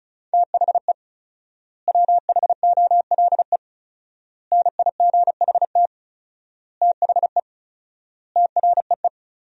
Morse code, heard as a keyed tone, about 36 wpm, 700 Hz. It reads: THE WHOLE NIGHT THE TREE
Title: 4x Word-Spacing - Sentences from Top 500 Words - 35wpm